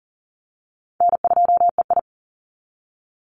Morse code, heard as a keyed tone, about 40 wpm, 700 Hz. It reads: N2EI
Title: Easy Call Signs Repeated 3x - 40wpm